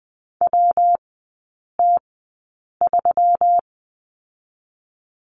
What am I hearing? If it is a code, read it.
Morse code, heard as a keyed tone, about 20 wpm, spaced as for 8 wpm, 700 Hz. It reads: WT3
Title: 121 Lesson 20e - Number 3 - Three Characters 20wpm